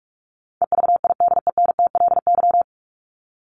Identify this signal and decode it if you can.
Morse code, heard as a keyed tone, about 44 wpm, 700 Hz. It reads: EVIDENTLY
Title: LY Suffix - Top 100 Words with Base Words 45wpm